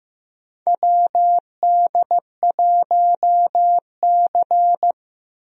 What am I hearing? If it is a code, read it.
Morse code, heard as a keyed tone, about 15 wpm, 700 Hz. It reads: WD1C
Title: USA Call Signs - 4 Characters - Repeated 2x - 15wpm